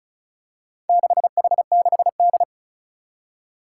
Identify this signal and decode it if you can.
Morse code, heard as a keyed tone, about 36 wpm, 700 Hz. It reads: 6H6D
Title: Four Challenging Letter-Number Combinations 35wpm